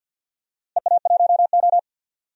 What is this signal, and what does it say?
Morse code, heard as a keyed tone, about 47 wpm, 700 Hz. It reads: EA1O
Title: DXCC-100 Call Signs - 4 Characters - Repeated 2x - 50wpm